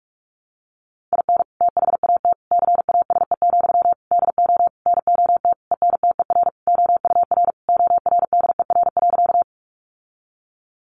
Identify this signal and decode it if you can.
Morse code, heard as a keyed tone, about 44 wpm, 700 Hz. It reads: IN THAT CASE, DO NOT ENTER OUR ORDER.
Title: Sentences from Top 1300 Words 45wpm